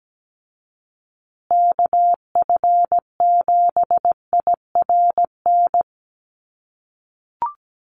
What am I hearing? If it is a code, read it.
Morse code, heard as a keyed tone, about 17 wpm, 700 Hz. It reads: KF7IRN
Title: USA Call Signs Repeated 2x 17wpm